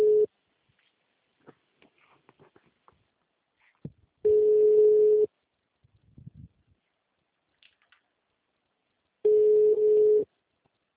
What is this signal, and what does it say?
Ringback tone (Europe, 425 Hz).
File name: out- -2011- - - .420